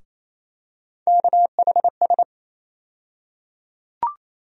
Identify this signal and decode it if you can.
Morse code, heard as a keyed tone, about 28 wpm, 700 Hz. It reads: KHS
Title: USA Call Sign Postfixes - 3 Letters 28wpm